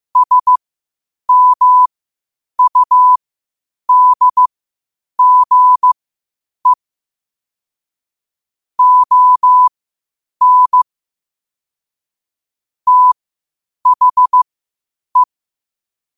Morse code, heard as a keyed tone, about 15 wpm, spaced as for 8 wpm, 1000 Hz. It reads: SMUDGE ON THE
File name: qotdfiles / 8wpm